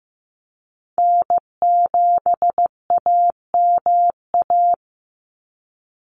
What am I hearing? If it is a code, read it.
Morse code, heard as a keyed tone, about 15 wpm, 700 Hz. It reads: N7AMA
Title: NA SOTA Chaser Callsigns 15wpm